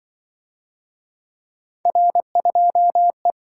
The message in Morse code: R2E

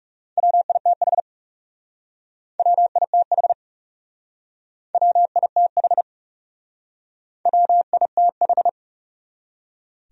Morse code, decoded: WITH WITH WITH WITH

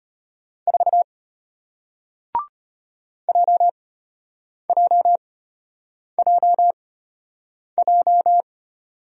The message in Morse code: 4 JJJJ

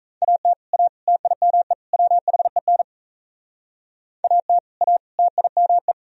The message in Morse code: AT A TIME WHEN AT A TIME

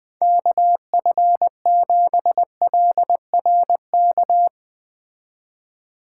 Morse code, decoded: KF7LRK

